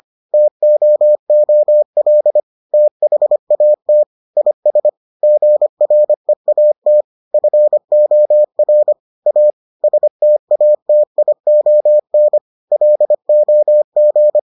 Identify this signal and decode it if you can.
Morse code, heard as a keyed tone, about 25 wpm, 600 Hz. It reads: TOOL THAT IS GREAT FOR A STATION LOG